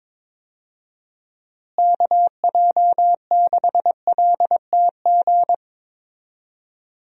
Morse code, heard as a keyed tone, about 22 wpm, 700 Hz. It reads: KJ6LTG
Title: USA Call Signs - 6 Characters - Repeated 2x - 22wpm